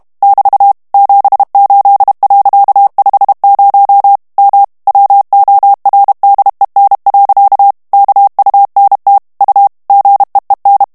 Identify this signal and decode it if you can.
Morse code, heard as a keyed tone, about 32 wpm, 800 Hz. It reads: <BT> 78.50 M WORDEN. KUNT U GEEN
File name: HOLLANDAIS texte clair 1 vitesse 30 mots minute